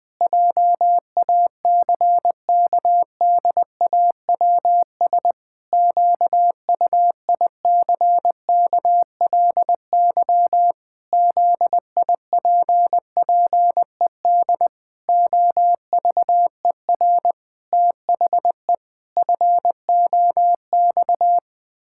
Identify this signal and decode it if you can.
Morse code, heard as a keyed tone, about 20 wpm, 700 Hz. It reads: JACKDAWS QUICKLY ZIPPED OVER THE FOX
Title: Pangram Challenge v2 - 20wpm